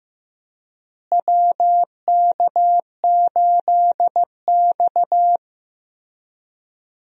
Morse code, heard as a keyed tone, about 15 wpm, 700 Hz. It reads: WK8X
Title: USA Call Signs - 4 Characters - Repeated 3x - 15wpm